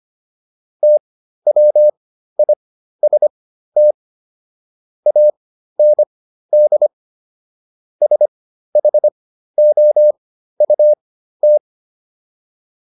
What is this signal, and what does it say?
Morse code, heard as a keyed tone, about 25 wpm, spaced as for 13 wpm, 600 Hz. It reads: TWIST AND SHOUT